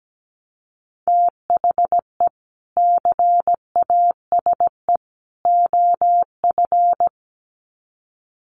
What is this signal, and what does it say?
Morse code, heard as a keyed tone, about 17 wpm, 700 Hz. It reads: THE CASE OF
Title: Sets of 3 Words 17wpm